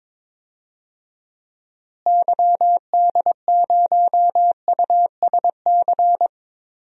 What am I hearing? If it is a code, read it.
Morse code, heard as a keyed tone, about 22 wpm, 700 Hz. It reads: YD0USC
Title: DXCC-100 Call Signs - 6 Characters - Repeated 2x - 22wpm